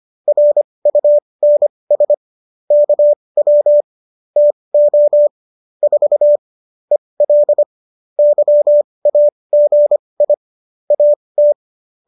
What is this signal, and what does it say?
Morse code, heard as a keyed tone, about 25 wpm, spaced as for 20 wpm, 590 Hz. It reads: RUNS KW TO 4 EL YAGI AT
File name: qso207 20